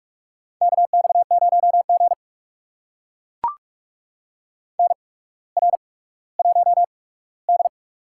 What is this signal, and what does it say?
Morse code, heard as a keyed tone, about 44 wpm, spaced as for 19 wpm, 700 Hz. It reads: KX0G NR1D
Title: USA Call Signs - 4 Characters - 8x Character Spacing - 45wpm